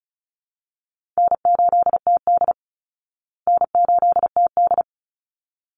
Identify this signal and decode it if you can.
Morse code, heard as a keyed tone, about 34 wpm, 700 Hz. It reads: N8TD N8TD
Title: Call Signs Repeated 3x - 35wpm